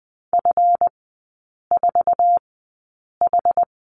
Morse code, heard as a keyed tone, about 20 wpm, spaced as for 8 wpm, 700 Hz. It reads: F4H
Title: 138 Lesson 23e - Letter f - Three Characters 20wpm